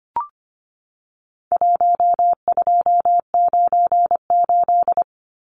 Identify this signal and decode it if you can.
Morse code, heard as a keyed tone, about 25 wpm, 700 Hz. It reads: 1298
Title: Numbers - 4-Digit 25wpm